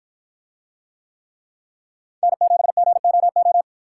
Morse code, heard as a keyed tone, about 53 wpm, 700 Hz. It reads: N7GOO